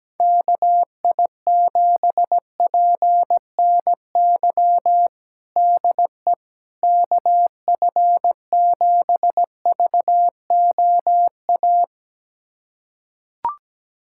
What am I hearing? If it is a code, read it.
Morse code, heard as a keyed tone, about 17 wpm, 700 Hz. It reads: KI7PNY DE KF7VOA
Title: USA Callsign DE Callsign - 6 Characters - 17wpm